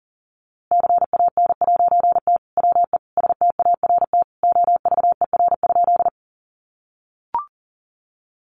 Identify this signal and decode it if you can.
Morse code, heard as a keyed tone, about 40 wpm, 700 Hz. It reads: CAN'T WE START OVER?
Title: A Sound Of Thunder 40wpm